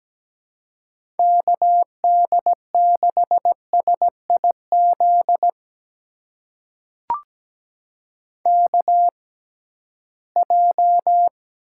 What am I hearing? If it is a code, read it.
Morse code, heard as a keyed tone, about 17 wpm, spaced as for 13 wpm, 700 Hz. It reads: KD6SIZ KJ
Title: USA Call Signs - 6 Characters - 6x Character Spacing - 17wpm